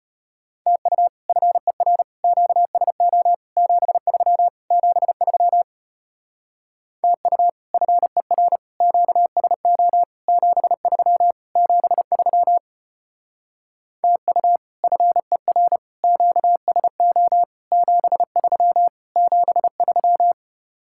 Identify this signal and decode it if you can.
Morse code, heard as a keyed tone, about 33 wpm, 700 Hz. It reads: TU FER QSO 73 73 TU FER QSO 73 73 TU FER QSO 73 73